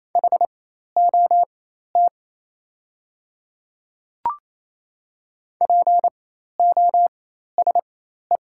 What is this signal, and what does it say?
Morse code, heard as a keyed tone, about 28 wpm, spaced as for 13 wpm, 700 Hz. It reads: HOT POSE
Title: 4x Character Spacing - Top 500 Words - 28wpm